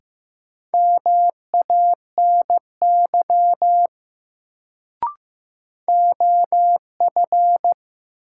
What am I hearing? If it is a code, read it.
Morse code, heard as a keyed tone, about 16 wpm, 700 Hz. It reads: MANY OF
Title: Sentences from Top 100 Words - New Word Drill - 15wpm